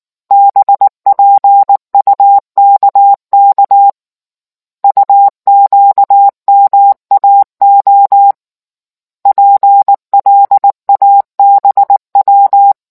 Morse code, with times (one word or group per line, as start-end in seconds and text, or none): 0.3-3.9 BPUKK
4.8-8.3 UQMAO
9.3-12.7 PLABW